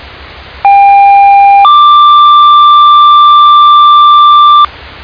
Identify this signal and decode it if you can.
Two-tone page, heard as a one-second tone, then a three-second tone, about 800 then 1180 Hz, with noise. It is heard from 0.6 to 4.7 s.